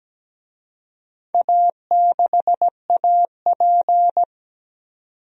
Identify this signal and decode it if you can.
Morse code, heard as a keyed tone, about 17 wpm, 700 Hz. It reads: A6AP
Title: DXCC-100 Call Signs - 4 Characters - 17wpm